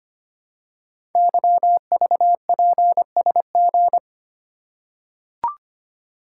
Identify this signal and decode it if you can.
Morse code, heard as a keyed tone, about 25 wpm, 700 Hz. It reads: YVPSG
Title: Code Groups - 5 Letters - 25wpm